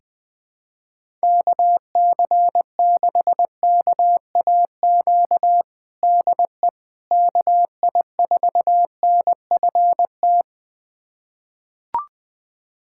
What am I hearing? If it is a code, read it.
Morse code, heard as a keyed tone, about 20 wpm, 700 Hz. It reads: KC6KAQ DE KI4NFT
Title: USA Callsign DE Callsign - 6 Characters - 20wpm